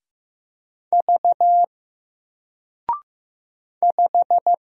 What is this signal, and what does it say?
Morse code, heard as a keyed tone, about 15 wpm, 700 Hz. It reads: V 5
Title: Single Letter-Number Alt 15wpm